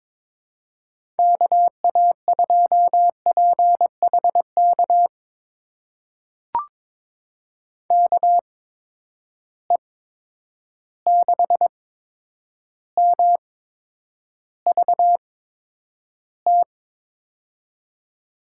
Morse code, heard as a keyed tone, about 22 wpm, spaced as for 9 wpm, 700 Hz. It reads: KA2PHK KE6MVT
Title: USA Call Signs - 6 Characters - 8x Character Spacing - 22wpm